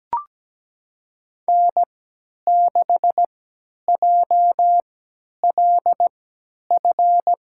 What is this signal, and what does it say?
Morse code, heard as a keyed tone, about 17 wpm, spaced as for 10 wpm, 700 Hz. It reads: N6JLF
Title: USA Call Signs - 3x Character Spacing - 17wpm